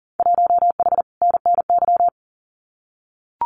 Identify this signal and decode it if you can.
Morse code, heard as a keyed tone, about 40 wpm, 700 Hz. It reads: 1H NNY
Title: Winter Field Day Exchanges 40wpm